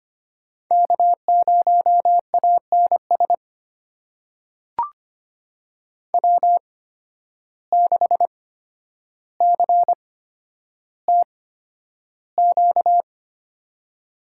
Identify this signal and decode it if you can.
Morse code, heard as a keyed tone, about 25 wpm, spaced as for 11 wpm, 700 Hz. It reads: K0ANS W6CTQ
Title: USA Call Signs - 5 Characters - 8x Character Spacing - 25wpm